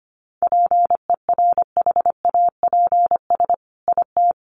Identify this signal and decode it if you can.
Morse code, heard as a keyed tone, about 25 wpm, 700 Hz. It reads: PERHAPS IT
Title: Sentences from Top 1500 Words 25wpm